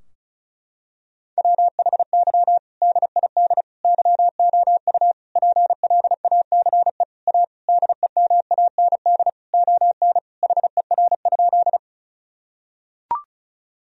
Morse code, heard as a keyed tone, about 36 wpm, 700 Hz. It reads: WHY DID YOU PLACE A DEMAND ON HER?
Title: Sentences from Top 1000 Words - Review Part 4 - 35wpm